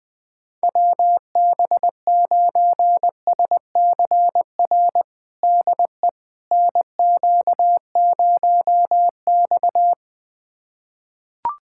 Morse code, heard as a keyed tone, about 21 wpm, 700 Hz. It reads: WB9SCR DE NQ0X
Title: USA Callsign DE Callsign 20wpm